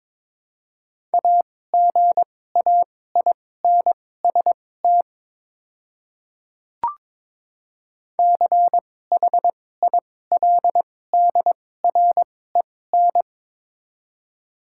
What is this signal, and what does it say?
Morse code, heard as a keyed tone, about 22 wpm, spaced as for 16 wpm, 700 Hz. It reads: AGAINST CHILDREN
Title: 2x Character Spacing - Top 500 Words - 22wpm